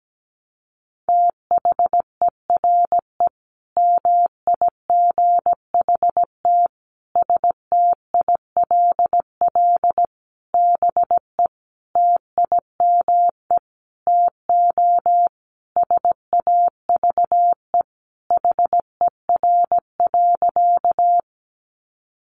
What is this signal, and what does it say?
Morse code, heard as a keyed tone, about 17 wpm, 700 Hz. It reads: THERE MIGHT STILL BE TIME TO SAVE HER.
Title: Sentences from Top 1100 Words 17wpm